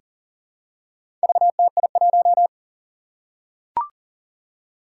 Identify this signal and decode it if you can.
Morse code, heard as a keyed tone, about 41 wpm, 700 Hz. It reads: VTI1
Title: Four Challenging Letter-Number Combinations 40wpm